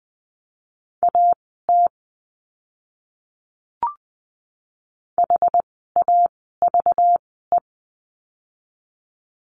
Morse code, heard as a keyed tone, about 20 wpm, spaced as for 15 wpm, 700 Hz. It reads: AT HAVE
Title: 199 Lesson 32f - Letter k - Words 20wpm